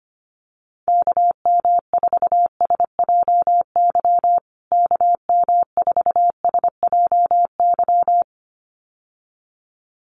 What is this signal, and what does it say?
Morse code, heard as a keyed tone, about 25 wpm, 700 Hz. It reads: KM4SJY KM4SJY
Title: USA Call Signs Repeated 2x 25wpm